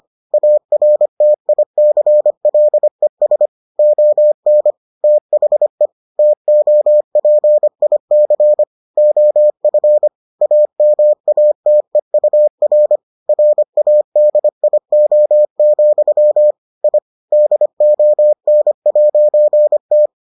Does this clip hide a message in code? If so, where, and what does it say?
Yes: Morse code from 0.3 to 20.1 s, heard as a keyed tone, about 25 wpm, 600 Hz. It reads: ARTICLES ON THE TOPIC OF AMATEUR RADIO, I DON'T